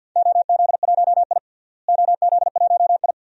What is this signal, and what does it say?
Morse code, heard as a keyed tone, about 52 wpm, 700 Hz. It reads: OZ1I OZ1I
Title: DXCC-100 Call Signs - 4 Characters - Repeated 2x - 50wpm